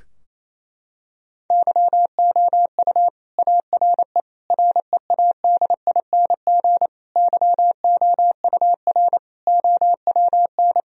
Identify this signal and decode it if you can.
Morse code, heard as a keyed tone, about 27 wpm, 700 Hz. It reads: YOU ARE READING YOUR OWN
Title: Sentences from Top 1000 Words - Review Part 3 - 28wpm